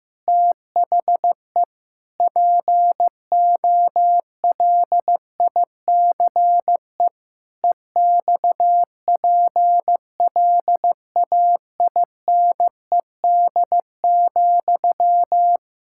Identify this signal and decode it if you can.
Morse code, heard as a keyed tone, about 15 wpm, 700 Hz. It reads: THE POLICE EXPLAINED,
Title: Pangram Challenge v2 - 15wpm